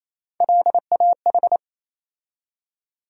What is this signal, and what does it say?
Morse code, heard as a keyed tone, about 28 wpm, 700 Hz. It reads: LAH
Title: USA Call Sign Postfixes - 3 Letters 28wpm